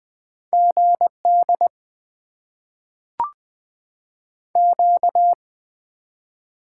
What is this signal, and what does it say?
Morse code, heard as a keyed tone, about 20 wpm, 700 Hz. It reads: GD Q
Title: Top 90 QSO Elements - 8x Character Spacing - 20wpm